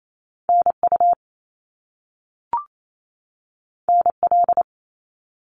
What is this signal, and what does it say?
Morse code, heard as a keyed tone, about 28 wpm, 700 Hz. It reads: NU NL
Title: Canadian Provinces and Territories 28wpm